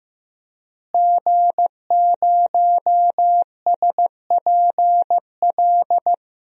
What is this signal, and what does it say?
Morse code, heard as a keyed tone, about 16 wpm, 700 Hz. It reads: G0SPL
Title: DXCC-100 Call Signs - 5 Characters - Repeated 3x - 15wpm